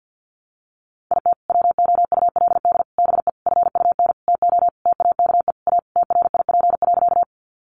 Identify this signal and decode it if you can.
Morse code, heard as a keyed tone, about 50 wpm, 700 Hz. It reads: IT WOULD BE FUN TO TAKE A TRIP.
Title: Sentences from Top 1300 Words 50wpm